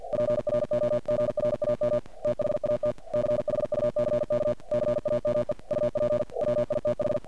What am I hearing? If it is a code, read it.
Morse code, heard as a keyed tone, about 49 wpm, 600 Hz. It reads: PROGRAM THAT CHUCK CAME UP WITH
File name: archived / CRSnet